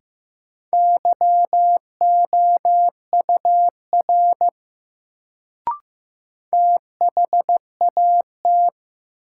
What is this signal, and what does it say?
Morse code, heard as a keyed tone, about 15 wpm, 700 Hz. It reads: YOUR THAT
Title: Sentences from Top 100 Words - New Word Drill - 15wpm